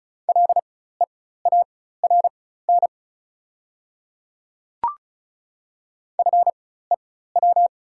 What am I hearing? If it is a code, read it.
Morse code, heard as a keyed tone, about 36 wpm, spaced as for 16 wpm, 690 Hz. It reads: LEARN FEW
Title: 4x Character Spacing - Top 300 Words - 35wpm